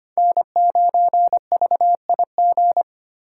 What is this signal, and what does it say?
Morse code, heard as a keyed tone, about 25 wpm, 700 Hz. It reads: N9VIG
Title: USA Call Signs - 4x Character Spacing - 25wpm